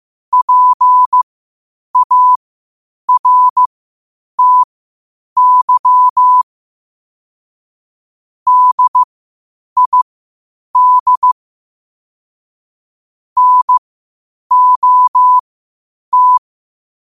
Morse code, heard as a keyed tone, about 15 wpm, spaced as for 8 wpm, 1000 Hz. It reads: PARTY DID NOT
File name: 8wpm